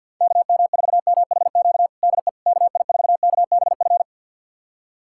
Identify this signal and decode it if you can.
Morse code, heard as a keyed tone, about 49 wpm, 700 Hz. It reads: KM4MHQ DE KI4KBF
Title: USA Callsign DE Callsign - 6 Characters - 50wpm